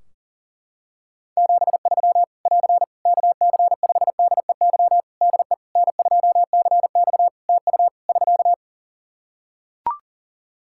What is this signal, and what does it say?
Morse code, heard as a keyed tone, about 37 wpm, 700 Hz. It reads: 73 <AR> KC5DEY DE N2CX TU <SK>
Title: Practice QSO - Exchange 4 - Ending - 40wpm